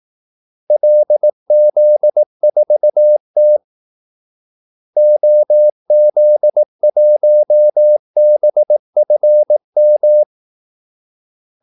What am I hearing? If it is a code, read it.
Morse code, heard as a keyed tone, about 18 wpm, 600 Hz. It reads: LZ4T OZ1BFM
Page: Kaldesignaler 18 wpm